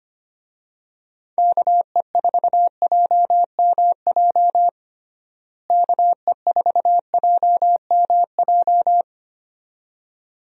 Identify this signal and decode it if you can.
Morse code, heard as a keyed tone, about 25 wpm, 700 Hz. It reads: KE4JMJ KE4JMJ
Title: USA Call Signs - 6 Characters - Repeated 2x - 25wpm